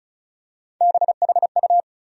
Morse code, heard as a keyed tone, about 34 wpm, 700 Hz. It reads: BHU
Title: USA Call Sign Postfixes - 3 Letters 35wpm